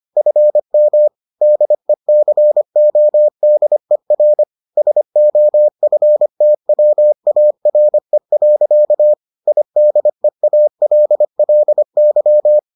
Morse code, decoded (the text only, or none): FM DECODER SOFTWARE. IDEALLY